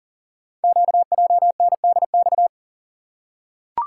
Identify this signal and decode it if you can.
Morse code, heard as a keyed tone, about 39 wpm, 700 Hz. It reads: QJNDX
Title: Code Groups - 5 Letters - 40wpm